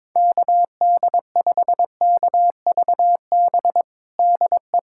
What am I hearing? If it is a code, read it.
Morse code, heard as a keyed tone, about 22 wpm, 700 Hz. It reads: KD5KVB DE
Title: USA Callsign DE Callsign - 6 Characters - 22wpm